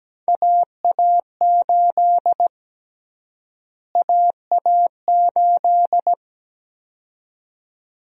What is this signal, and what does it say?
Morse code, heard as a keyed tone, about 17 wpm, 700 Hz. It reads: AA8 AA8
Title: USA Call Sign Prefixes - Repeated 2x - 17wpm